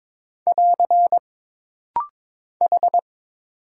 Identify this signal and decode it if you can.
Morse code, heard as a keyed tone, about 22 wpm, 700 Hz. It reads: <AR> H
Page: This rapid-fire timing is geared to push your ICR (Instant Character Recognition) skills.